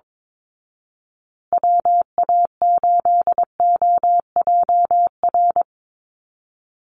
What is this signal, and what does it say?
Morse code, heard as a keyed tone, about 22 wpm, 700 Hz. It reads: WA8OJR
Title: Call Signs Repeated 2x - 22wpm